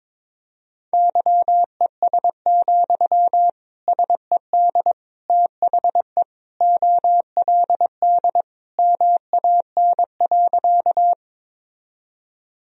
Morse code, heard as a keyed tone, about 22 wpm, 700 Hz. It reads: YES, SED THE OLD MAN.
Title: Sentences from Top 500 Words and Abbreviations 22wpm